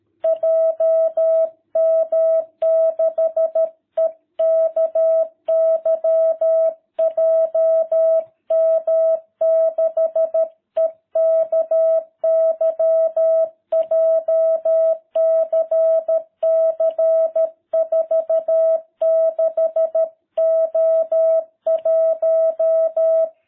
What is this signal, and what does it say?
Morse code, heard as a keyed tone, about 13 wpm, 660 Hz. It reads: JM6EKY JM6EKY JCC 46O1